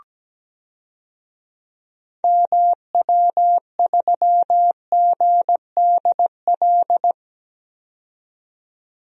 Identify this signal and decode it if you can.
Morse code, heard as a keyed tone, about 17 wpm, 700 Hz. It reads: MW3GDL
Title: DXCC-100 Call Signs - 6 Characters - 17wpm